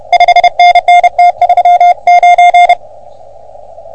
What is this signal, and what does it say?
Morse code, heard as a keyed tone, about 29 wpm, 710 Hz. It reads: 5NN T39
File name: OM3PA PILUP CUC 388